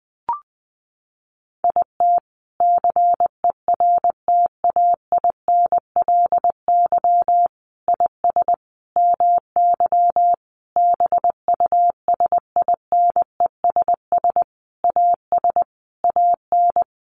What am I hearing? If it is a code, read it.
Morse code, heard as a keyed tone, about 20 wpm, 700 Hz. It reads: IT CERTAINLY IS MY BUSINESS AS AN
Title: Sentences from Top 1200 Words 20wpm